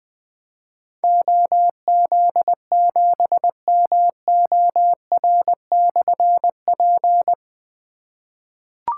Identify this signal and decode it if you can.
Morse code, heard as a keyed tone, about 20 wpm, 700 Hz. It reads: OZ7MOR/P
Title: Difficult Call Signs 20wpm